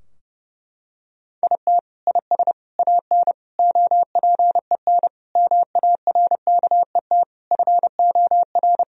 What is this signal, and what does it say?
Morse code, heard as a keyed tone, about 30 wpm, 700 Hz. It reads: IT IS AN OPEN MARKET FOR
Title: Sentences from Top 1000 Words - Review Part 3 - 30wpm